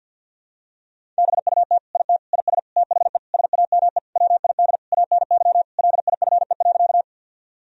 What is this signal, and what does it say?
Morse code, heard as a keyed tone, about 49 wpm, 700 Hz. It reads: BUT IT IS THE SAME WID ANY LIFE.
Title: Sentences from Top 300 Words and Abbreviations 50wpm